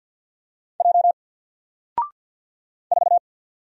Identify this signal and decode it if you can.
Morse code, heard as a keyed tone, about 50 wpm, 700 Hz. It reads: J 4
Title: Single Letter-Number 50wpm